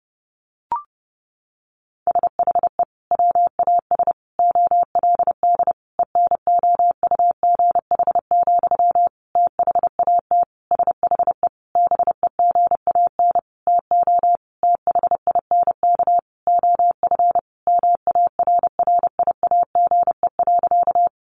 Morse code, heard as a keyed tone, about 30 wpm, 700 Hz. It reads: SHE WAS OLD ENOUGH, THAT SHE BEGAN TO THINK OF MARRIAGE.
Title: Sentences from Top 800 Words 30wpm